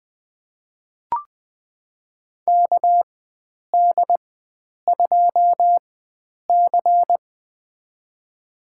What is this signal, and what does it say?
Morse code, heard as a keyed tone, about 21 wpm, spaced as for 9 wpm, 700 Hz. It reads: KD2C
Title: USA Call Signs - 4 Characters - 4x Character Spacing - 20wpm